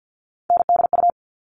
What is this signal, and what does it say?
Morse code, heard as a keyed tone, about 50 wpm, 700 Hz. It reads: NDU
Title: All Cut Numbers - 3-Digit 50wpm